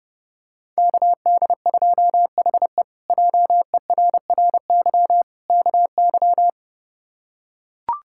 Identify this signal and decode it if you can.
Morse code, heard as a keyed tone, about 30 wpm, 700 Hz. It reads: KD2HE JERRY KY
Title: K1USN SST QSO Practice - 30wpm